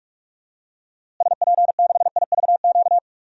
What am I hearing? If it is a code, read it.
Morse code, heard as a keyed tone, about 41 wpm, 700 Hz. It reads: SP6IVQ
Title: DXCC-100 Call Signs - 6 Characters - Repeated 2x - 45wpm